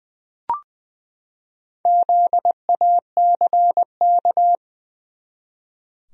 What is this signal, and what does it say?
Morse code, heard as a keyed tone, about 20 wpm, 700 Hz. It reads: ZACK
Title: USA Common Names - Max 4 Letters - 20wpm